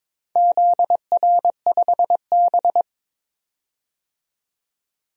Morse code, heard as a keyed tone, about 22 wpm, 700 Hz. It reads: ZR5B
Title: DXCC-100 Call Signs - 4 Characters - 22wpm